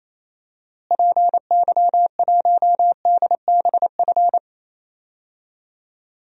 Morse code, heard as a keyed tone, about 28 wpm, 700 Hz. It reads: PY1DBF
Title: DXCC-100 Call Signs - 6 Characters - Repeated 2x - 28wpm